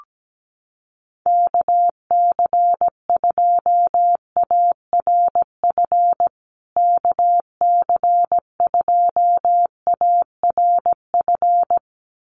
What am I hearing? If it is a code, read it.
Morse code, heard as a keyed tone, about 17 wpm, 700 Hz. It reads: KC2ARF KC2ARF
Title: USA Call Signs Repeated 2x 17wpm